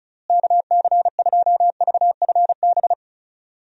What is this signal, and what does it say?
Morse code, heard as a keyed tone, about 36 wpm, 700 Hz. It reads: KC2VFB
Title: USA Call Signs - 6 Characters - Repeated 2x - 35wpm